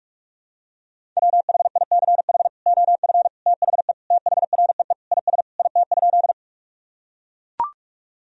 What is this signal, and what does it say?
Morse code, heard as a keyed tone, about 42 wpm, 700 Hz. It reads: WHICH OF THE THREE IS IT?
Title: Sentences from Top 1000 Words - Review Part 1 - 45wpm